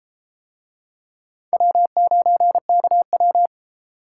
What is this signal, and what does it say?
Morse code, heard as a keyed tone, about 31 wpm, 700 Hz. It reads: W9KW